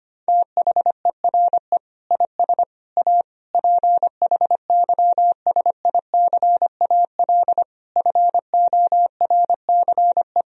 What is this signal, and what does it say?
Morse code, heard as a keyed tone, about 25 wpm, 700 Hz. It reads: THERE IS A PHYSICAL FORCE
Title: Sentences from Top 1000 Words - Review Part 4 - 25wpm